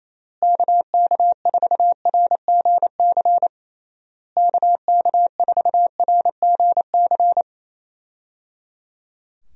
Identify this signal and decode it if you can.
Morse code, heard as a keyed tone, about 28 wpm, 700 Hz. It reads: KK4RGC KK4RGC
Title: USA Call Signs - 6 Characters - Repeated 3x - 28wpm